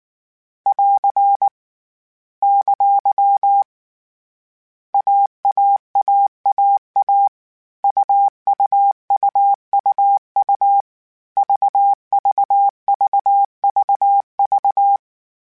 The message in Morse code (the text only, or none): <AR> ! AAAAA UUUUU VVVVV